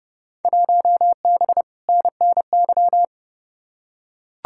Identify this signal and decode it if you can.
Morse code, heard as a keyed tone, about 32 wpm, 700 Hz. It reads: 1B NNY
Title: Field Day Exchanges - 30wpm